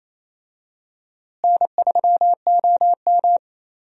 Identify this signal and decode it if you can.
Morse code, heard as a keyed tone, about 28 wpm, 700 Hz. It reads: N3OM
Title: USA Call Signs - 4 Characters - Repeated 3x - 28wpm